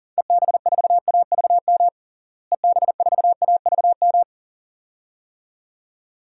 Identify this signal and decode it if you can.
Morse code, heard as a keyed tone, about 40 wpm, 700 Hz. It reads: EB4AVM EB4AVM
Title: DXCC-100 Call Signs - 6 Characters - Repeated 2x - 40wpm